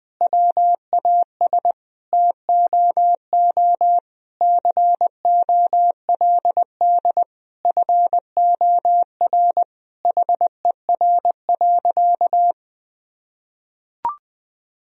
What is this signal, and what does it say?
Morse code, heard as a keyed tone, about 20 wpm, 700 Hz. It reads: WAS TOO COLD FOR HER.
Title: Sentences from Top 1000 Words - Review Part 4 - 20wpm